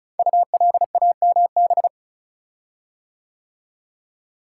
Morse code, heard as a keyed tone, about 36 wpm, 700 Hz. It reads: ULAMB
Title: Code Groups - 5 Letters - 35wpm